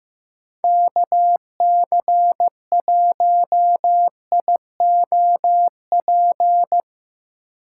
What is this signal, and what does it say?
Morse code, heard as a keyed tone, about 15 wpm, 700 Hz. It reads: KC1IOP
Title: USA Call Signs - 3x Character Spacing - 15wpm